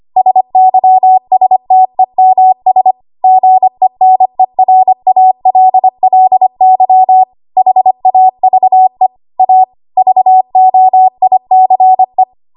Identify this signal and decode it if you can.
Morse code, heard as a keyed tone, about 25 wpm, 750 Hz. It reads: SYSTEMS GENERALLY HAVE A VOICE
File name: Morse / 25WPM